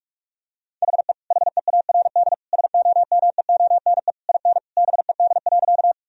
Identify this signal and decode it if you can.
Morse code, heard as a keyed tone, about 46 wpm, 700 Hz. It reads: HE HEARD SOMEONE IN BED.
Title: Sentences from Top 1000 Words - Review Part 4 - 45wpm